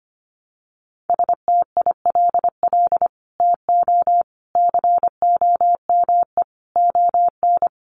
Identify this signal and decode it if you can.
Morse code, heard as a keyed tone, about 25 wpm, 700 Hz. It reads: STILL TO COME ON
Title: Sets of 5 Words Encore2 25wpm